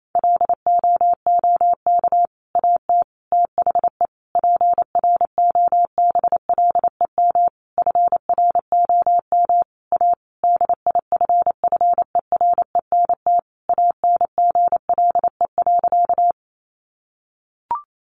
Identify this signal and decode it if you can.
Morse code, heard as a keyed tone, about 28 wpm, 700 Hz. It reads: LOOK AT THE PROBLEM FROM A DIFFERENT ANGLE.
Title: Sentences from Top 1500 Words 28wpm